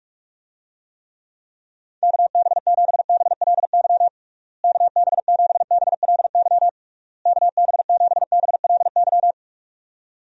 Morse code, heard as a keyed tone, about 46 wpm, 700 Hz. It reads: KB7BLY KB7BLY KB7BLY
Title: USA Call Signs - 6 Characters - Repeated 3x - 45wpm